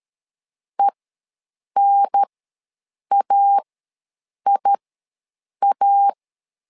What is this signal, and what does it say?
Morse code, heard as a keyed tone, about 13 wpm, spaced as for 7 wpm, 800 Hz. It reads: ENAIA
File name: LEZIONI / 2 UHDN / 2 UHDN ws 12wpm 1708